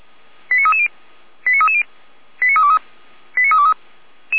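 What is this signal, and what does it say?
Five-tone selective call (ZVEI), heard as quick runs of five tones, tones of about 70 ms. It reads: 89330 89330 89323 89323
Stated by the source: ZVEI five-tone selective call received with an SDR.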